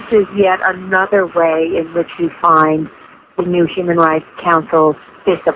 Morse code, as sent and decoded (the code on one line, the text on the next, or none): none